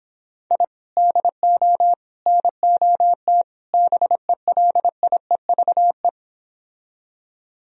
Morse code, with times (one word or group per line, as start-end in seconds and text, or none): 0.5-0.6 I
1.0-1.9 DO
2.3-3.4 NOT
3.7-6.1 BELIEVE